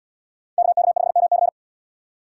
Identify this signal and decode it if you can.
Morse code, heard as a keyed tone, about 71 wpm, 700 Hz. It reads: KK5MC